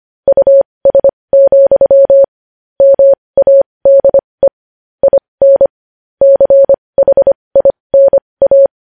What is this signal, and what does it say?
Morse code, heard as a keyed tone, about 25 wpm, spaced as for 20 wpm, 550 Hz. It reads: US, MADE IN CHINA